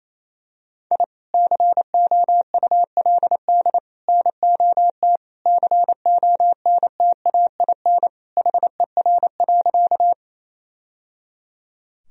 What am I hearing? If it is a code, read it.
Morse code, heard as a keyed tone, about 28 wpm, 700 Hz. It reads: I COULD NOT CONTAIN HER.
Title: Sentences from Top 1000 Words - Review Part 2 - 28wpm